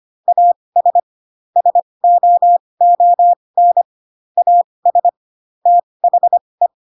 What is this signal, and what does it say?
Morse code, heard as a keyed tone, about 25 wpm, spaced as for 20 wpm, 700 Hz. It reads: AS SOON AS THE